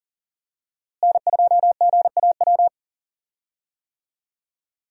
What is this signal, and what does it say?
Morse code, heard as a keyed tone, about 40 wpm, 700 Hz. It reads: N2GAW
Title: USA Call Signs - 5 Characters - Repeated 3x - 40wpm